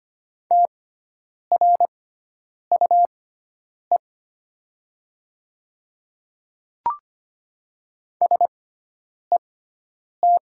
Morse code, heard as a keyed tone, about 25 wpm, spaced as for 9 wpm, 700 Hz. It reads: TRUE SET
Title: 6x Character Spacing - Top 500 Words - 25wpm